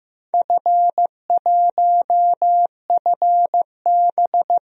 Morse code, heard as a keyed tone, about 15 wpm, 700 Hz. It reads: F1FB
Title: DXCC-100 Call Signs - 4 Characters - 15wpm